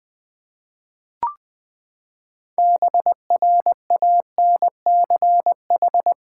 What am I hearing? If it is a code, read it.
Morse code, heard as a keyed tone, about 20 wpm, 700 Hz. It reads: BRANCH
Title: Six Letter Words - 20wpm